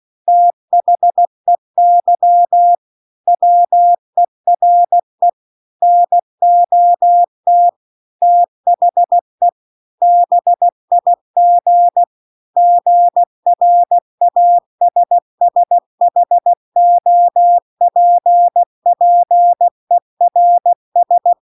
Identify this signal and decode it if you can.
Morse code, heard as a keyed tone, about 16 wpm, 700 Hz. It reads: THEY WERE NOT THE BIG GRASSHOPPERS